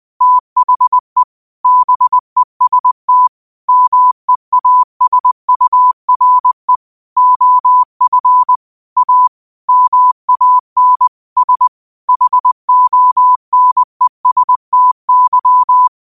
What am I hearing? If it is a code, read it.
Morse code, heard as a keyed tone, about 20 wpm, 1000 Hz. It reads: THE BEST MEASURE OF A MAN S HONESTY
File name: qotdfiles / 20wpm